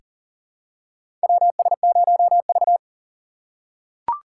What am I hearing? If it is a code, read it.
Morse code, heard as a keyed tone, about 40 wpm, 700 Hz. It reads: WS0V
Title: USA Call Signs - 4 Characters - 3x Character Spacing - 40wpm